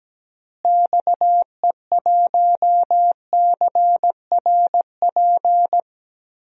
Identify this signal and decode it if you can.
Morse code, heard as a keyed tone, about 17 wpm, 700 Hz. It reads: XE1CRP
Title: DXCC-100 Call Signs - 6 Characters - Repeated 3x - 17wpm